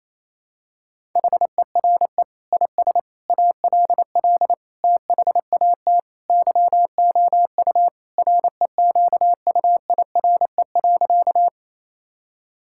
Morse code, heard as a keyed tone, about 28 wpm, 700 Hz. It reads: HERE IS ALL THAT YOU REQUIRE.
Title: Sentences from Top 1000 Words - Review Part 6 - 28wpm